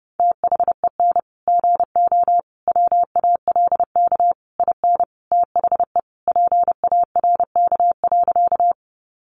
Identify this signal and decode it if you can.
Morse code, heard as a keyed tone, about 28 wpm, 700 Hz. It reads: THEN GO WALK IN THE PARK.
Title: Sentences from Top 1300 Words 30wpm